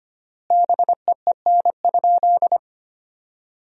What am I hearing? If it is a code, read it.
Morse code, heard as a keyed tone, about 25 wpm, 700 Hz. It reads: BEEN?
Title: 3x Word-Spacing - Sentences from Top 400 Words - 25wpm